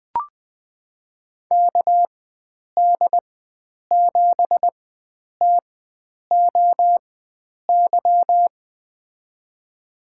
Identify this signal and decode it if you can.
Morse code, heard as a keyed tone, about 20 wpm, spaced as for 9 wpm, 700 Hz. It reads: KD7TOY
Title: USA Call Signs - 4x Character Spacing - 20wpm